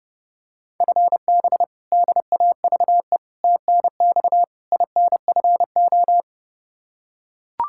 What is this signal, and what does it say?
Morse code, heard as a keyed tone, about 30 wpm, 700 Hz. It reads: FB DAVE TNX INFO
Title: K1USN SST QSO Practice 30wpm